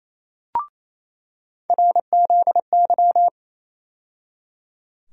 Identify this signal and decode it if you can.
Morse code, heard as a keyed tone, about 27 wpm, 700 Hz. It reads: RZY
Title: USA Call Sign Postfixes - 3 Letters 28wpm